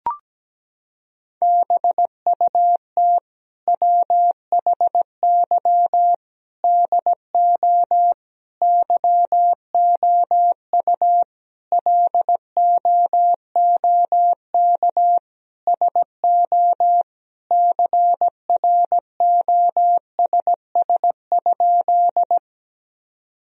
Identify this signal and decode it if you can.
Morse code, heard as a keyed tone, about 17 wpm, 700 Hz. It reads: BUT WHY DO YOU LOOK SO CROSS?
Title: Sentences from Top 1000 Words - Review Part 1 - 17wpm